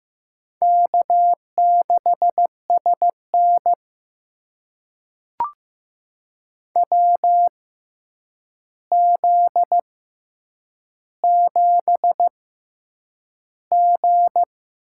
Morse code, heard as a keyed tone, about 15 wpm, spaced as for 8 wpm, 700 Hz. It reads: K6SN WZ7G
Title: USA Call Signs - 4 Characters - 6x Character Spacing - 15wpm